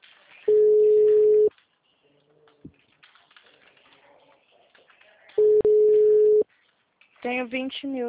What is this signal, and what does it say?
Ringback tone (Europe, 425 Hz), with speech.
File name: out- -2002- - - .57718